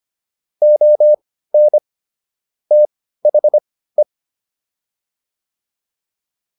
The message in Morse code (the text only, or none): ON THE